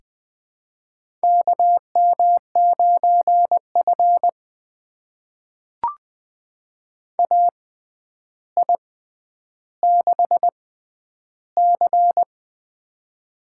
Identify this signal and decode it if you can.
Morse code, heard as a keyed tone, about 20 wpm, spaced as for 10 wpm, 700 Hz. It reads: KM9F AI6C
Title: USA Call Signs - 4 Characters - 6x Character Spacing - 20wpm